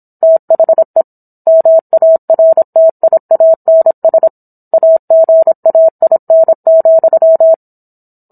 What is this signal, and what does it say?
Morse code, heard as a keyed tone, about 27 wpm, 650 Hz. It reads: THE MARTIANS AGAIN,